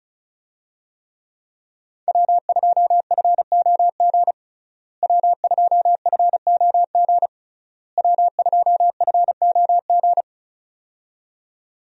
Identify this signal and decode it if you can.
Morse code, heard as a keyed tone, about 36 wpm, 700 Hz. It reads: W2FOG W2FOG W2FOG
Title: USA Call Signs - 5 Characters - Repeated 3x - 35wpm